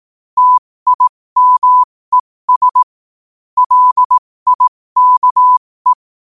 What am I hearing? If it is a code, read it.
Morse code, heard as a keyed tone, about 18 wpm, spaced as for 15 wpm, 1000 Hz. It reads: TIMES LIKE
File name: qotdfiles / 15wpm